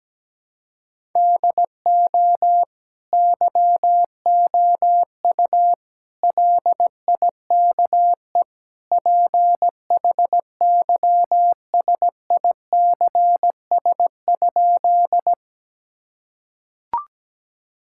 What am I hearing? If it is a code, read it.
Morse code, heard as a keyed tone, about 17 wpm, 700 Hz. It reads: DO YOU LIKE PHYSICS?
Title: Sentences from Top 1000 Words - Review Part 5 - 17wpm